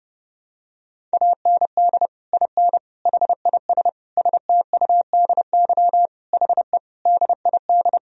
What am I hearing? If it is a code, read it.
Morse code, heard as a keyed tone, about 29 wpm, 700 Hz. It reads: AND IN HIS STUDY HE DID
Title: Sentences from Top 1000 Words - Review Part 1 - 30wpm